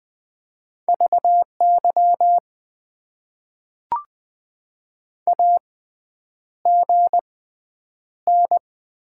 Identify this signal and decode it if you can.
Morse code, heard as a keyed tone, about 20 wpm, spaced as for 9 wpm, 700 Hz. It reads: VY AGN
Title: Top 90 QSO Elements - 6x Character Spacing - 20wpm